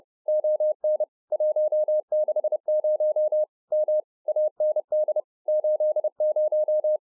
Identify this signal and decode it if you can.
Morse code, heard as a keyed tone, about 30 wpm, 610 Hz. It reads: ON 160 M AND 80